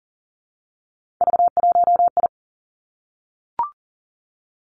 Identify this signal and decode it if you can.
Morse code, heard as a keyed tone, about 42 wpm, 700 Hz. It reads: V1I
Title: Three Challenging Letter-Number Combinations 40wpm